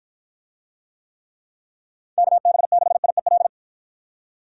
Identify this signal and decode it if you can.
Morse code, heard as a keyed tone, about 54 wpm, 700 Hz. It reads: KB6IEL